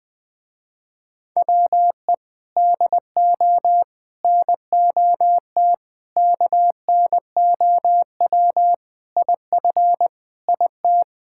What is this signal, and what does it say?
Morse code, heard as a keyed tone, about 21 wpm, 690 Hz. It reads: WE DO NOT KNOW IF IT
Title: Sentences with 4-Word Phrases - Spoken Beginning - 20wpm